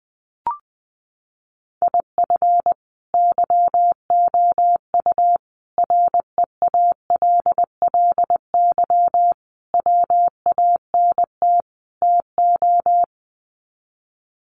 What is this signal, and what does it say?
Morse code, heard as a keyed tone, about 20 wpm, 700 Hz. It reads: IF YOU REALLY WANT TO
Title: Sets of 5 Words Encore2 20wpm